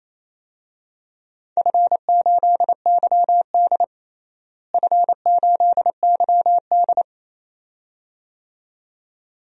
Morse code, heard as a keyed tone, about 28 wpm, 700 Hz. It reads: F8YD F8YD
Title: DXCC-100 Call Signs - 4 Characters - Repeated 2x - 28wpm